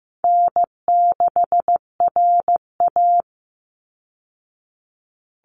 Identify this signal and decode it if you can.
Morse code, heard as a keyed tone, about 15 wpm, 700 Hz. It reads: N6RA
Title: Easy Call Signs Repeated 3x - 15wpm